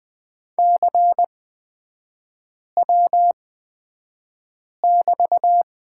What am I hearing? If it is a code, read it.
Morse code, heard as a keyed tone, about 20 wpm, spaced as for 5 wpm, 700 Hz. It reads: CW<BT>
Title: 236 Lesson 38e - Letter q - Three Characters 20wpm